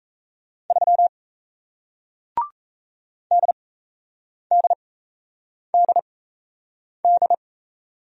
Morse code, heard as a keyed tone, about 38 wpm, spaced as for 8 wpm, 700 Hz. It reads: 3 DDDD